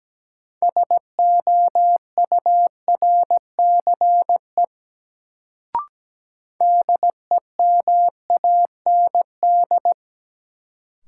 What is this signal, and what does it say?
Morse code, heard as a keyed tone, about 17 wpm, 700 Hz. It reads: SOURCE DEMAND
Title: Sentences from Top 800 Words - New Word Drill - 17wpm